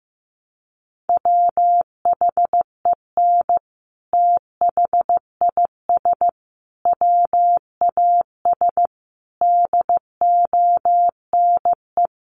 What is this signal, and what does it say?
Morse code, heard as a keyed tone, about 15 wpm, 700 Hz. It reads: WHEN THIS WAS DONE
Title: Sentences from Top 900 Words 15wpm